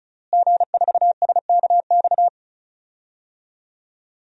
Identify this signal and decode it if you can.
Morse code, heard as a keyed tone, about 34 wpm, 700 Hz. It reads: G4SKX
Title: DXCC-100 Call Signs - 5 Characters - Repeated 2x - 35wpm